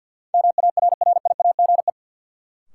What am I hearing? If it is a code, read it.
Morse code, heard as a keyed tone, about 51 wpm, 700 Hz. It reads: MARRIAGE
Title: Sentences from Top 800 Words - New Word Drill - 50wpm